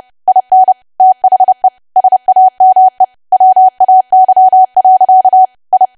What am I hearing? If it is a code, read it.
Morse code, heard as a keyed tone, about 30 wpm, 750 Hz. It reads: IN THE SAME WAY. I
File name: cwa / W1AW-Level3 / 30WPM